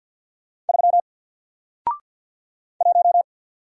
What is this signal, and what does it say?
Morse code, heard as a keyed tone, about 48 wpm, 700 Hz. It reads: 3 1
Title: Single Letter-Number Alt 50wpm